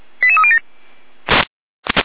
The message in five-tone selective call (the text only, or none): 88307